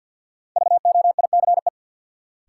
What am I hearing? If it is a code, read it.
Morse code, heard as a keyed tone, about 52 wpm, 700 Hz. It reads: VOICE
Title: Sentences from Top 500 Words - New Word Drill - 50wpm